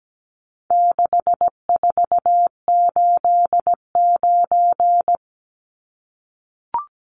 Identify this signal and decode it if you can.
Morse code, heard as a keyed tone, about 17 wpm, 700 Hz. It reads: 6489
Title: Numbers - 4-Digit 17wpm